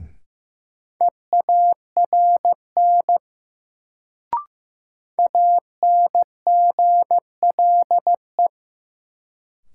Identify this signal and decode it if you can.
Morse code, heard as a keyed tone, about 15 wpm, 700 Hz. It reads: EARN ANGLE
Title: Sentences from Top 1500 Words - New Word Drill - 15wpm